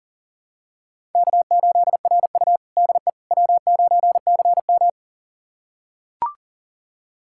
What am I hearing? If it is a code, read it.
Morse code, heard as a keyed tone, about 40 wpm, 700 Hz. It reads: K8RU DE W9CM
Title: USA Callsign DE Callsign - 4 Characters - 40wpm